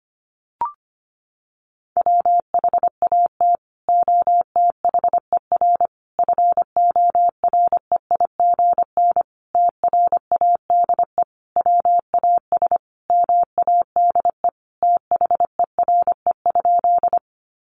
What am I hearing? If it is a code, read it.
Morse code, heard as a keyed tone, about 25 wpm, 700 Hz. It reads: WHAT OTHER FOREIGN TRADE WAS MADE THERE?
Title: Sentences from Top 1400 Words 25wpm